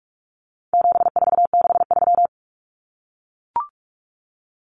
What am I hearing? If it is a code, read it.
Morse code, heard as a keyed tone, about 44 wpm, 700 Hz. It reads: 7463
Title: Common Cut Numbers - 4-Digit 45wpm